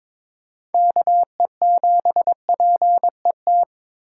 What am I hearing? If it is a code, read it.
Morse code, heard as a keyed tone, about 22 wpm, 700 Hz. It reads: KE7PET
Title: USA Call Signs - 6x Character Spacing - 22wpm